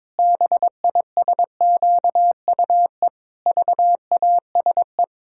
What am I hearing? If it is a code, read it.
Morse code, heard as a keyed tone, about 22 wpm, 700 Hz. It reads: BISQUE VASE
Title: Pangram Challenge v2 - 22wpm